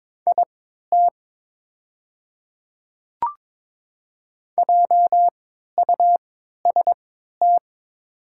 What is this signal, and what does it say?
Morse code, heard as a keyed tone, about 23 wpm, spaced as for 13 wpm, 690 Hz. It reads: IT JUST
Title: 3x Character Spacing - Top 500 Words - 22wpm